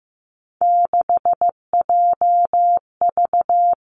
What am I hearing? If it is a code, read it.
Morse code, heard as a keyed tone, about 15 wpm, 700 Hz. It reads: 6JV
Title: Three Challenging Letter-Number Combinations 15wpm